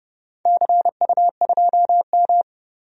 Morse code, heard as a keyed tone, about 29 wpm, 700 Hz. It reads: CU2M
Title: DXCC-100 Call Signs - 4 Characters - 30wpm